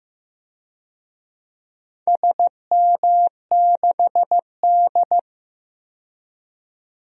Morse code, heard as a keyed tone, about 15 wpm, 700 Hz. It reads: SM6D
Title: DXCC-100 Call Signs - 4 Characters - 15wpm